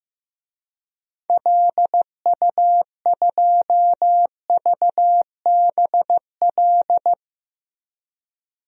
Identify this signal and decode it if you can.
Morse code, heard as a keyed tone, about 16 wpm, 700 Hz. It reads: LU2VBL
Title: DXCC-100 Call Signs - 6 Characters - 15wpm